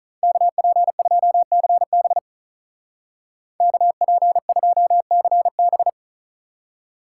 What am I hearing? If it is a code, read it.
Morse code, heard as a keyed tone, about 44 wpm, 700 Hz. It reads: KP2CB KP2CB